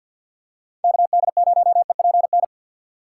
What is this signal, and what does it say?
Morse code, heard as a keyed tone, about 50 wpm, 700 Hz. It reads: KD0EPN
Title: USA Call Signs - 6 Characters - 3x Character Spacing - 50wpm